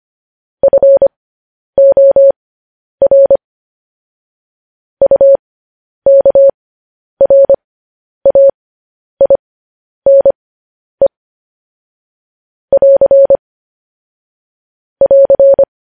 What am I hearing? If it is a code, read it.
Morse code, heard as a keyed tone, about 25 wpm, spaced as for 10 wpm, 550 Hz. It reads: FOR UKRAINE <AR> <AR>
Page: news-headlines-in-morse-code-at-10-wpm